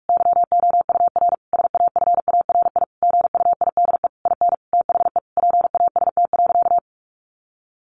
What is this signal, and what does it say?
Morse code, heard as a keyed tone, about 44 wpm, 700 Hz. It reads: YOUR SAFARI GUIDE IN THE PAST.
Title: A Sound Of Thunder 45wpm